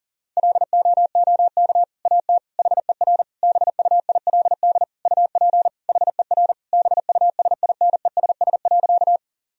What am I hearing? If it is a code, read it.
Morse code, heard as a keyed tone, about 40 wpm, 700 Hz. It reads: LOOK AT HER BUILD UP HER BUSINESS.
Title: Sentences from Top 1000 Words - Review Part 1 - 40wpm